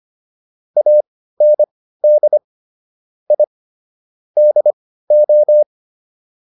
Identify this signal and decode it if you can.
Morse code, heard as a keyed tone, about 25 wpm, spaced as for 15 wpm, 600 Hz. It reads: AND I DO